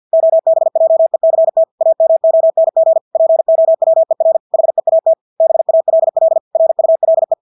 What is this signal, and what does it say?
Morse code, heard as a keyed tone, about 50 wpm, 650 Hz. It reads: OBJECT AMONG POWER HEAT BALL RULE
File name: Engelske ord 50 wpm